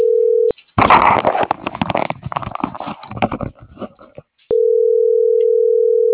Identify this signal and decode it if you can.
Ringback tone (North America), 440 + 480 Hz.